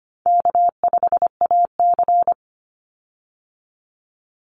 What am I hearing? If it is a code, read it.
Morse code, heard as a keyed tone, about 25 wpm, 700 Hz. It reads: K5AC
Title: USA Call Signs 25wpm